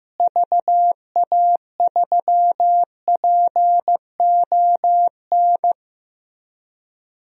Morse code, heard as a keyed tone, about 15 wpm, 700 Hz. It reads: VA3PON
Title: DXCC-100 Call Signs - 6 Characters - Repeated 3x - 15wpm